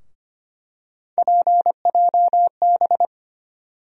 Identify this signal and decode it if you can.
Morse code, heard as a keyed tone, about 25 wpm, 700 Hz. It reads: PJB
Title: USA Call Sign Postfixes - 3 Letters 25wpm